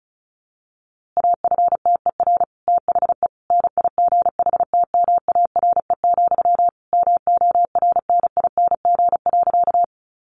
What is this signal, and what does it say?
Morse code, heard as a keyed tone, about 34 wpm, 700 Hz. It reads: AFTER THE NIGHTMARE, MORNING.
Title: A Sound Of Thunder 35wpm